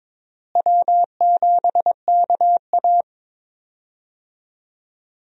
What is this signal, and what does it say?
Morse code, heard as a keyed tone, about 22 wpm, 700 Hz. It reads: W7KA
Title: USA Call Signs - 4 Characters - Repeated 2x - 22wpm